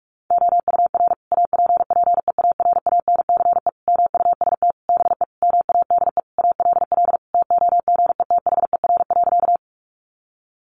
Morse code, heard as a keyed tone, about 44 wpm, 700 Hz. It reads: OUR APPEARANCE MUST BE MADE ALL TOGETHER.
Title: Sentences from Top 1100 Words 45wpm